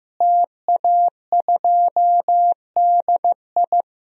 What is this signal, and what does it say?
Morse code, heard as a keyed tone, about 15 wpm, 700 Hz. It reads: TA2DI
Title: DXCC-100 Call Signs - 5 Characters - Repeated 2x - 15wpm